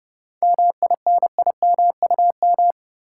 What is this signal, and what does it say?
Morse code, heard as a keyed tone, about 30 wpm, 700 Hz. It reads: MINIMUM
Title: Seven Letter Words - 30wpm